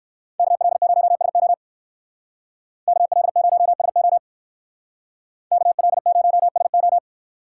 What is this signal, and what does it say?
Morse code, heard as a keyed tone, about 58 wpm, 690 Hz. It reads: KL0SO KL0SO KL0SO